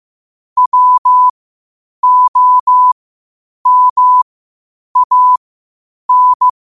Morse code, heard as a keyed tone, about 15 wpm, spaced as for 8 wpm, 1000 Hz. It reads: WOMAN
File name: qotdfiles / 8wpm